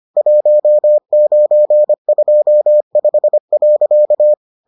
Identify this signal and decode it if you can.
Morse code, heard as a keyed tone, about 25 wpm, 600 Hz. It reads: 1925.